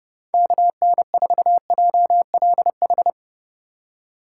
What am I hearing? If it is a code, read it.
Morse code, heard as a keyed tone, about 30 wpm, 700 Hz. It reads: KN4JLH